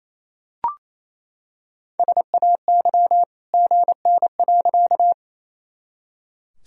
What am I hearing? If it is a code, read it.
Morse code, heard as a keyed tone, about 28 wpm, 700 Hz. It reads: SAY GN.
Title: Sentences from Top 300 Words and Abbreviations 28wpm